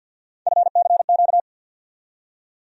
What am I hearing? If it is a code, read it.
Morse code, heard as a keyed tone, about 53 wpm, 700 Hz. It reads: FCQ
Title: USA Call Sign Postfixes - 3 Letters 50wpm